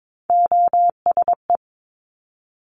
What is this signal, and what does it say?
Morse code, heard as a keyed tone, about 22 wpm, 700 Hz. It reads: OSE
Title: Top Three Letter Combinations in Words 22wpm